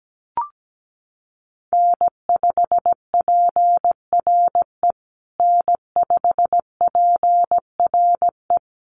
Morse code, heard as a keyed tone, about 17 wpm, 700 Hz. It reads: N5PRE N5PRE
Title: USA Call Signs Repeated 2x 17wpm